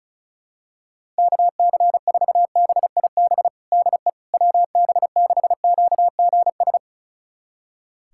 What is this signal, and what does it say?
Morse code, heard as a keyed tone, about 36 wpm, 700 Hz. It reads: KC4BIB DE WB6QGS
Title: USA Callsign DE Callsign - 6 Characters - 35wpm